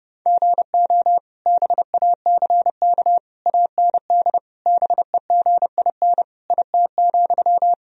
Morse code, heard as a keyed tone, about 30 wpm, 700 Hz. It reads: GO BACK AND BEGIN IT,
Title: Sentences from Top 1000 Words - Review Part 1 - 30wpm